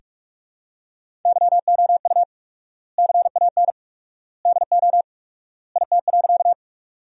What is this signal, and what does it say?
Morse code, heard as a keyed tone, about 43 wpm, 700 Hz. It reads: YOU CAN DO IT.
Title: 4x Word-Spacing - Sentences from Top 100 Words - 45wpm